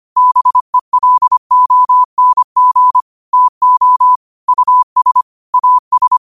Morse code, heard as a keyed tone, about 25 wpm, 1000 Hz. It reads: BELONG TO US AS
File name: qotdfiles / 25wpm